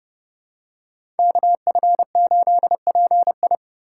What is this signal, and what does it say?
Morse code, heard as a keyed tone, about 31 wpm, 700 Hz. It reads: KF8PI
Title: USA Call Signs - 5 Characters - 4x Character Spacing - 30wpm